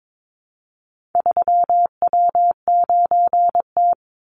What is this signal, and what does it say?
Morse code, heard as a keyed tone, about 22 wpm, 700 Hz. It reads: 3W9T
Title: Call Signs - 22wpm